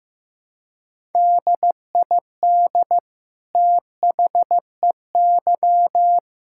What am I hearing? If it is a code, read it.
Morse code, heard as a keyed tone, about 15 wpm, 700 Hz. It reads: DID THEY
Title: Sentences from Top 1000 Words - Review Part 5 - 15wpm